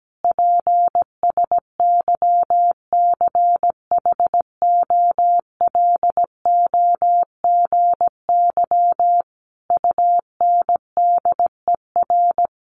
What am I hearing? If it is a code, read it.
Morse code, heard as a keyed tone, about 17 wpm, 700 Hz. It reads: PSYCHOLOGY UNDER
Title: Sentences from Top 1400 Words 17wpm